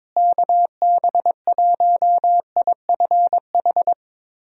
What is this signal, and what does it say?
Morse code, heard as a keyed tone, about 21 wpm, 700 Hz. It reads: KB1IFH
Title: USA Call Signs - 6 Characters - Repeated 3x - 22wpm